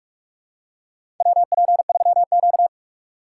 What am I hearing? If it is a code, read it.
Morse code, heard as a keyed tone, about 46 wpm, 700 Hz. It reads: WP3Q
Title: USA Call Signs - 4 Characters - 6x Character Spacing - 45wpm